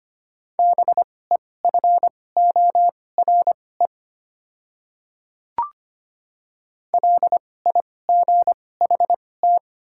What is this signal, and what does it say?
Morse code, heard as a keyed tone, about 25 wpm, spaced as for 18 wpm, 690 Hz. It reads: BEFORE LIGHT
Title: 2x Character Spacing - Top 500 Words - 25wpm